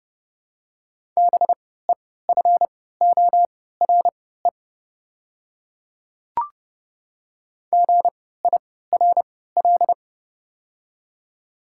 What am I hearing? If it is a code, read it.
Morse code, heard as a keyed tone, about 31 wpm, spaced as for 17 wpm, 700 Hz. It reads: BEFORE GIRL
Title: 3x Character Spacing - Top 500 Words - 30wpm